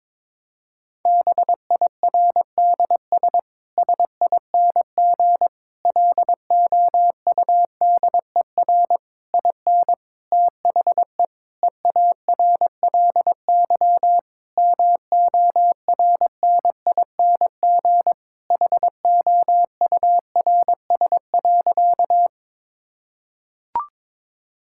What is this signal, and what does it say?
Morse code, heard as a keyed tone, about 22 wpm, 700 Hz. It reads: BIRDS SING LOUDER IN THE EARLY MORNING HOURS.
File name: Difficult Sentences with Hints - 22wpm